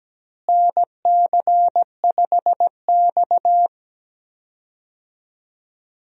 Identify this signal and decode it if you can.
Morse code, heard as a keyed tone, about 17 wpm, 700 Hz. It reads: NC5X
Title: USA Call Signs - 4 Characters - Repeated 3x - 17wpm